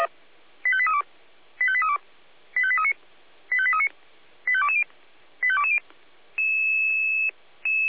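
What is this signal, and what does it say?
Five-tone selective call (ZVEI), heard as quick runs of five tones, tones of about 70 ms. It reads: 85832 85832 85839 85839 85220 85220
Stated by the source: ZVEI five-tone selective call 85882 received on SDR.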